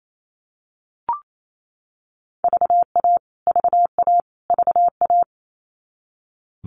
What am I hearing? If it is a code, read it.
Morse code, heard as a keyed tone, about 28 wpm, 700 Hz. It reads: VA VA VA
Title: US States Repeated 3x 28wpm